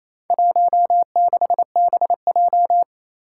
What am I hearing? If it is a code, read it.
Morse code, heard as a keyed tone, about 29 wpm, 700 Hz. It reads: 16BJ
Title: Four Challenging Letter-Number Combinations 28wpm